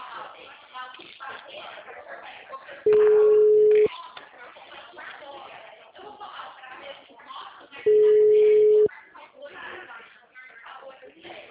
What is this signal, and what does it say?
Ringback tone (Europe, 425 Hz), with speech.